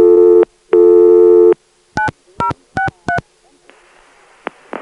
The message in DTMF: 9063